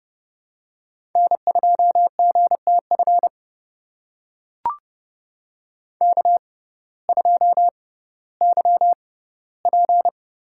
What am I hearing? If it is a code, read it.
Morse code, heard as a keyed tone, about 30 wpm, spaced as for 17 wpm, 700 Hz. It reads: N2GTF K2YP
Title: USA Call Signs - 6x Character Spacing - 30wpm